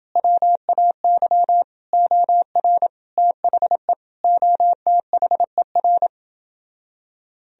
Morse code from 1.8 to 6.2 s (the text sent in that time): OR THE OTHER